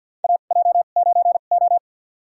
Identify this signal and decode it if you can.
Morse code, heard as a keyed tone, about 52 wpm, spaced as for 37 wpm, 700 Hz. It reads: AJ9O
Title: USA Call Signs - 4 Characters - 2x Character Spacing - 50wpm